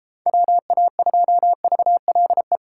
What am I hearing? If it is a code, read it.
Morse code, heard as a keyed tone, about 33 wpm, 700 Hz. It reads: WA2VLE
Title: Speed Racing - USA 2x3 Call Signs 28wpm